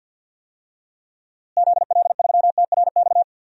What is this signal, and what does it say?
Morse code, heard as a keyed tone, about 52 wpm, 700 Hz. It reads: ZL3TRX
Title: DXCC-100 Call Signs - 6 Characters - Repeated 3x - 50wpm